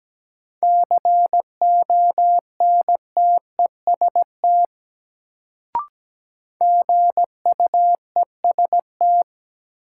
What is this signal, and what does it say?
Morse code, heard as a keyed tone, about 17 wpm, 700 Hz. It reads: CONTEST GUEST
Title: Sentences from Top 1600 Words - New Word Drill Alt - 17wpm